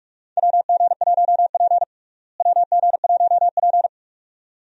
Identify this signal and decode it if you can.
Morse code, heard as a keyed tone, about 46 wpm, 700 Hz. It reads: WG1P WG1P
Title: USA Call Signs - 4 Characters - Repeated 2x - 45wpm